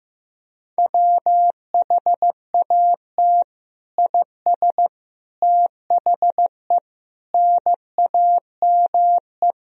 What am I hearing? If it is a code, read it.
Morse code, heard as a keyed tone, about 15 wpm, 700 Hz. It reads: WHAT IS THE NAME
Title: Sentences from Top 1000 Words - Review Part 1 - 15wpm